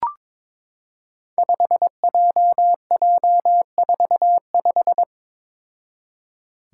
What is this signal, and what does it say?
Morse code, heard as a keyed tone, about 22 wpm, 700 Hz. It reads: 5JJ45
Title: Five Challenging Letter-Number Combinations 22wpm